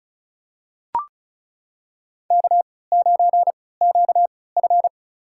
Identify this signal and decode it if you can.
Morse code, heard as a keyed tone, about 36 wpm, spaced as for 20 wpm, 700 Hz. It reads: K9QF
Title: USA Call Signs - 4 Characters - 3x Character Spacing - 35wpm